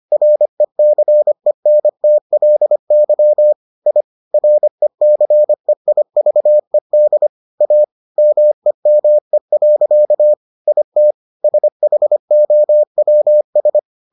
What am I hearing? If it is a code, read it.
Morse code, heard as a keyed tone, about 25 wpm, 600 Hz. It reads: RECENTLY I RECEIVED A MEME. IT SHOWS